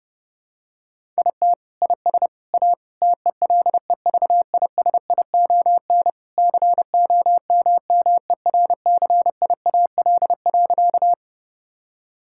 Morse code, heard as a keyed tone, about 30 wpm, 700 Hz. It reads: IT IS A TELEVISION COMMERCIAL.
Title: Sentences from Top 1000 Words - Review Part 5 - 30wpm